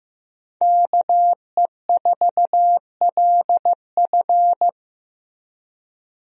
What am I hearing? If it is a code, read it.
Morse code, heard as a keyed tone, about 15 wpm, 700 Hz. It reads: KE4LF
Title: USA Call Signs - 5 Characters - Repeated 2x - 15wpm